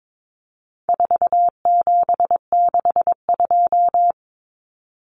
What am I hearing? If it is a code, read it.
Morse code, heard as a keyed tone, about 22 wpm, 700 Hz. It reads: 4762
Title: Numbers - 4-Digit 22wpm